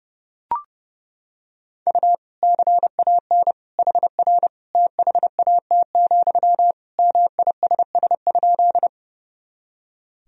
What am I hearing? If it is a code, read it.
Morse code, heard as a keyed tone, about 32 wpm, 700 Hz. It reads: U CAN HR THAT, MISS?
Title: Sentences from Top 500 Words and Abbreviations 30wpm